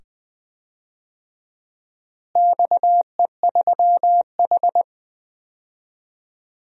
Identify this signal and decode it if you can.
Morse code, heard as a keyed tone, about 19 wpm, 700 Hz. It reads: XE3H
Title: DXCC-100 Call Signs - 4 Characters - Repeated 2x - 20wpm